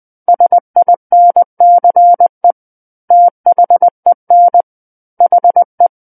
Morse code, heard as a keyed tone, about 20 wpm, 700 Hz. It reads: SINCE THEN HE
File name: 20 WPM